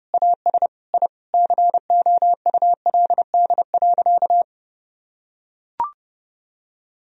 Morse code, heard as a keyed tone, about 30 wpm, 700 Hz. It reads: AS I COULD.
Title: Sentences from Top 1000 Words - Review Part 3 - 30wpm